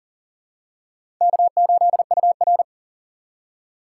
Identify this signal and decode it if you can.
Morse code, heard as a keyed tone, about 37 wpm, 700 Hz. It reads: K8UR
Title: USA Call Signs - 4 Characters - 6x Character Spacing - 40wpm